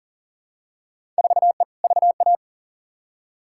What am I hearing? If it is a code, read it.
Morse code, heard as a keyed tone, about 37 wpm, 700 Hz. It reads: 4E VA
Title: Field Day Exchanges Encore - 40wpm